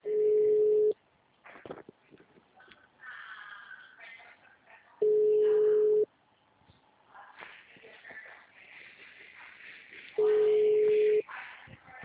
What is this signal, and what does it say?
Ringback tone (Europe, 425 Hz).